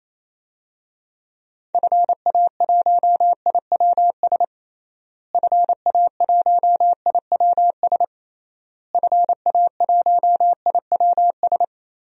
Morse code, heard as a keyed tone, about 28 wpm, 700 Hz. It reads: FA1IWS FA1IWS FA1IWS
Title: DXCC-100 Call Signs - 6 Characters - Repeated 3x - 28wpm